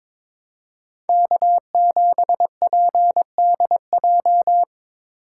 Morse code, heard as a keyed tone, about 22 wpm, 700 Hz. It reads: K7PDJ
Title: USA Call Signs - 5 Characters - 3x Character Spacing - 22wpm